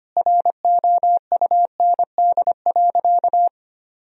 Morse code, heard as a keyed tone, about 25 wpm, 700 Hz. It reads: ROUND.
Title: Sentences from Top 1000 Words - Review Part 1 - 25wpm